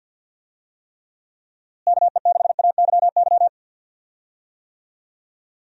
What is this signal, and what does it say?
Morse code, heard as a keyed tone, about 46 wpm, 700 Hz. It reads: KE6AYY
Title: USA Call Signs - 6 Characters - Repeated 2x - 50wpm